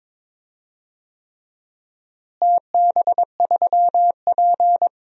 Morse code, heard as a keyed tone, about 22 wpm, 700 Hz. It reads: TB3P
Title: DXCC-100 Call Signs - 4 Characters - Repeated 3x - 22wpm